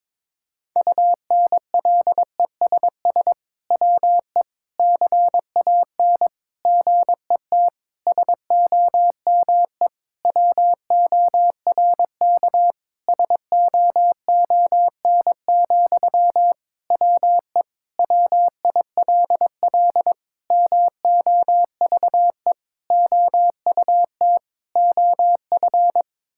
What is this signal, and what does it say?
Morse code, heard as a keyed tone, about 22 wpm, 700 Hz. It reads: UNLESS WE CAN GET SOME WORK SOON, WE WILL MOVE OUT OF
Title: Sentences from Top 1000 Words - Review Part 6 - 22wpm